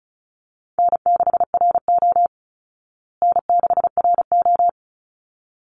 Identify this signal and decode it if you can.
Morse code, heard as a keyed tone, about 34 wpm, 700 Hz. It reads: N6RO N6RO
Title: Call Signs Repeated 3x - 35wpm